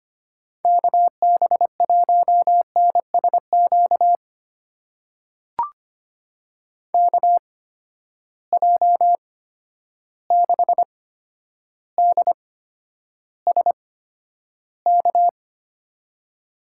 Morse code, heard as a keyed tone, about 25 wpm, spaced as for 11 wpm, 700 Hz. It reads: KB1NSQ KJ6DSK
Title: USA Call Signs - 6 Characters - 8x Character Spacing - 25wpm